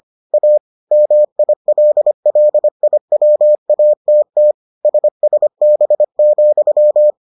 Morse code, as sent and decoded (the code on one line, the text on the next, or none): .- / -- .. .-.. .-.. .. .-- .- - - / ... ... -... --..--
A MILLIWATT SSB,